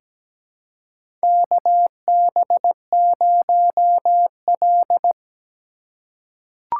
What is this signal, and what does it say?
Morse code, heard as a keyed tone, about 17 wpm, 700 Hz. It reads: KB0L
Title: USA Call Signs - 4 Characters - 3x Character Spacing - 17wpm